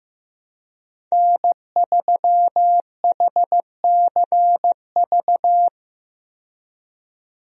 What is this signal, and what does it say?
Morse code, heard as a keyed tone, about 15 wpm, 700 Hz. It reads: N3HCV
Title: USA Call Signs - 5 Characters - 15wpm